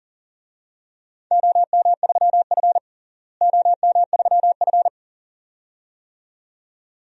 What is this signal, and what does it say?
Morse code, heard as a keyed tone, about 40 wpm, 700 Hz. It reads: OM3F OM3F
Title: DXCC-100 Call Signs - 4 Characters - Repeated 2x - 40wpm